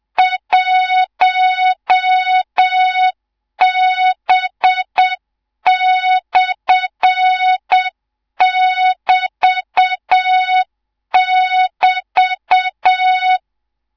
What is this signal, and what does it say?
Morse code, heard as a keyed tone, about 7 wpm, 3700 Hz, with noise. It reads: 1B/<BT><BT>
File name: Morze-HA2MN / hang / 18 IRASJELEK2